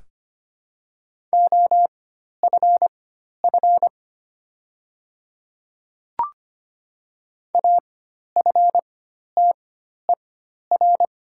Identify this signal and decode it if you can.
Morse code, heard as a keyed tone, about 25 wpm, spaced as for 12 wpm, 700 Hz. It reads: OFF AFTER
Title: 4x Character Spacing - Top 200 Words - 25wpm